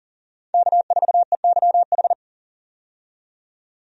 Morse code, heard as a keyed tone, about 40 wpm, 700 Hz. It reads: K4EYH
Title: USA Call Signs - 5 Characters - 40wpm